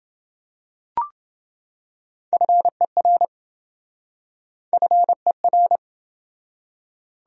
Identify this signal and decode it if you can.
Morse code, heard as a keyed tone, about 28 wpm, 700 Hz. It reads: FER FER